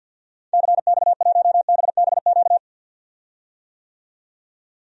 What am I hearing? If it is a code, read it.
Morse code, heard as a keyed tone, about 51 wpm, 700 Hz. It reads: CX1BBQ
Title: DXCC-100 Call Signs - 6 Characters - 50wpm